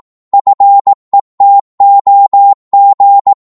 Morse code, heard as a keyed tone, about 18 wpm, 800 Hz. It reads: FETOG
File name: Morse / ltr-18wpm